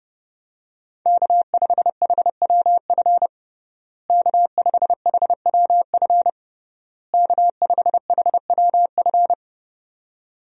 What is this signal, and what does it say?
Morse code, heard as a keyed tone, about 31 wpm, 700 Hz. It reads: K5HWF K5HWF K5HWF
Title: USA Call Signs - 5 Characters - Repeated 3x - 30wpm